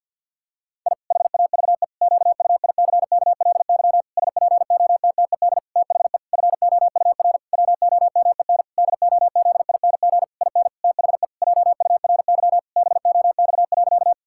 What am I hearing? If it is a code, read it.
Morse code, heard as a keyed tone, about 49 wpm, 700 Hz. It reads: I HAVE QUICKLY SPOTTED THE FOUR WOMEN DOZING IN THE JURY BOX.
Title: Pangram Challenge v2 - 50wpm